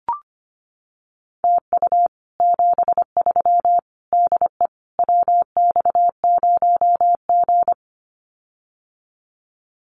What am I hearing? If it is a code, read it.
Morse code, heard as a keyed tone, about 25 wpm, 700 Hz. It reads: TU 73 DE WX0G
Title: Common QSO Phrases Encore 25wpm